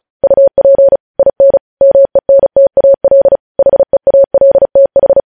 Morse code, heard as a keyed tone, about 36 wpm, 550 Hz. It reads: UP IN MENTAL HEALTH
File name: podcast / News headlines-1x-35x35